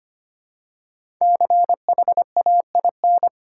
Morse code, heard as a keyed tone, about 25 wpm, 700 Hz. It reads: CHAIN
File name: 5 Letter Words - First 3 Letters Given - 25wpm